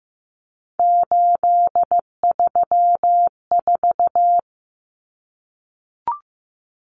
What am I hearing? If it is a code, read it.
Morse code, heard as a keyed tone, about 15 wpm, 700 Hz. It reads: 834
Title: Numbers - 3-Digit 15wpm